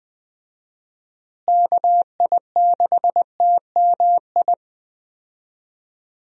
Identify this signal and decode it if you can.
Morse code, heard as a keyed tone, about 20 wpm, 700 Hz. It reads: KI6TMI
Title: USA Call Signs - 6 Characters - Repeated 3x - 20wpm